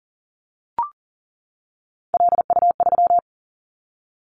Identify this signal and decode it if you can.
Morse code, heard as a keyed tone, about 38 wpm, 700 Hz. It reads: LU3
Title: Callsign Prefixes for Top 50 DXCC - 40wpm